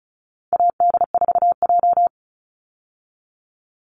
Morse code, heard as a keyed tone, about 34 wpm, 700 Hz. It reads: AD4J
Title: Call Signs Repeated 3x - 35wpm